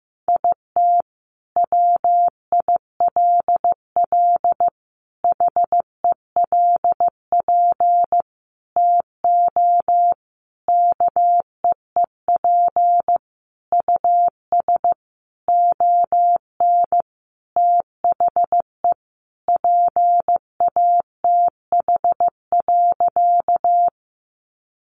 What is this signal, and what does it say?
Morse code, heard as a keyed tone, about 15 wpm, 700 Hz. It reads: IT WILL HELP TO KEEP US ON THE PATH.
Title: Sentences from Top 1400 Words 15wpm